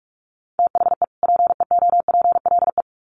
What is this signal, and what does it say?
Morse code, heard as a keyed tone, about 46 wpm, 700 Hz. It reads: THE PEOPLE
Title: Sets of 2 Words 45wpm